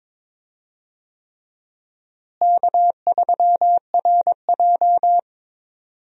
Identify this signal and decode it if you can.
Morse code, heard as a keyed tone, about 21 wpm, 700 Hz. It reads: K3RJ
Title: USA Call Signs - 4 Characters - Repeated 3x - 22wpm